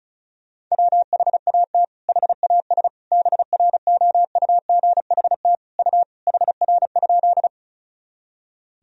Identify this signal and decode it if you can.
Morse code, heard as a keyed tone, about 34 wpm, 700 Hz. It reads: WHAT HAS BROUGHT U HR?
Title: Sentences from Top 500 Words and Abbreviations 35wpm